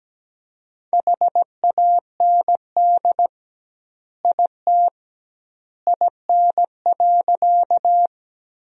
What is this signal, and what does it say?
Morse code, heard as a keyed tone, about 17 wpm, 700 Hz. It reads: HAND IT IN.
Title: 2x Word-Spacing - Sentences from Top 200 Words - 17wpm